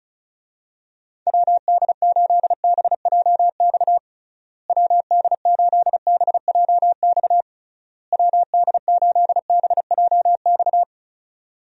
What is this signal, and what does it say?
Morse code, heard as a keyed tone, about 36 wpm, 700 Hz. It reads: WD8BJX WD8BJX WD8BJX
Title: USA Call Signs - 6 Characters - Repeated 3x - 35wpm